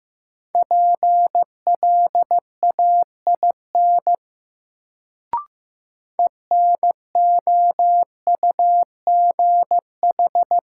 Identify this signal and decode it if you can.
Morse code, heard as a keyed tone, about 15 wpm, 700 Hz. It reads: PLAIN ENOUGH
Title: Sentences from Top 400 Words - New Word Drill - 15wpm